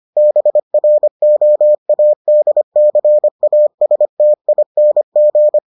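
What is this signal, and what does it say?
Morse code, heard as a keyed tone, about 25 wpm, 600 Hz. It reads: BROADCASTING
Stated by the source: Morse edition of the Foundations of Amateur Radio episodes.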